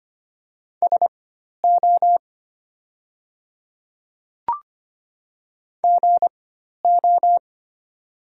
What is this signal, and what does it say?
Morse code, heard as a keyed tone, about 25 wpm, spaced as for 12 wpm, 700 Hz. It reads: SO GO
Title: 4x Character Spacing - Top 100 Words - 25wpm